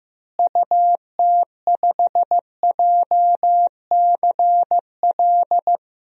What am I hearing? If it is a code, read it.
Morse code, heard as a keyed tone, about 15 wpm, 700 Hz. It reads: UT5JCL
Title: DXCC-100 Call Signs - 6 Characters - Repeated 2x - 15wpm